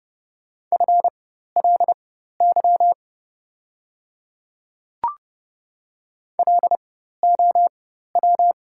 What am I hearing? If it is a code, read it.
Morse code, heard as a keyed tone, about 30 wpm, spaced as for 14 wpm, 690 Hz. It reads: FLY LOW
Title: 4x Character Spacing - Top 500 Words - 30wpm